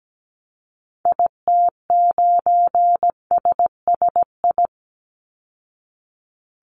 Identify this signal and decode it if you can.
Morse code, heard as a keyed tone, about 17 wpm, 700 Hz. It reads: IT9SSI
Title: Call Signs Repeated 3x - 17wpm